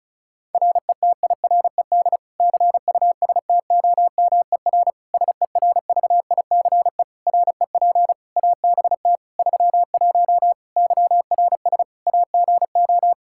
Morse code, heard as a keyed tone, about 36 wpm, 700 Hz. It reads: RETIRED CUSTOMER SERVICE REP ABT 31 YRS AGO
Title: Practice QSO - Exchange 3 - More Info - 35wpm